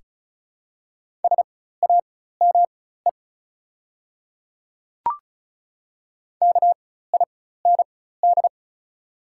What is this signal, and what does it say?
Morse code, heard as a keyed tone, about 35 wpm, spaced as for 16 wpm, 700 Hz. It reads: SAME KIND
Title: 4x Character Spacing - Top 300 Words - 35wpm